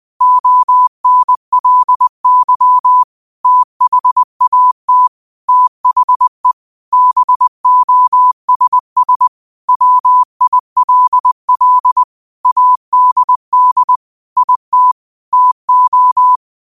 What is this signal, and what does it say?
Morse code, heard as a keyed tone, about 20 wpm, 1000 Hz. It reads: ONLY THAT THE BOSS WILL ADD IT TO
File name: qotdfiles / 20wpm